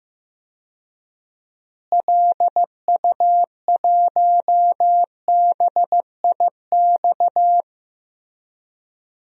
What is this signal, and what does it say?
Morse code, heard as a keyed tone, about 15 wpm, 700 Hz. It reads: LU1BIX
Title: DXCC-100 Call Signs - 6 Characters - Repeated 3x - 15wpm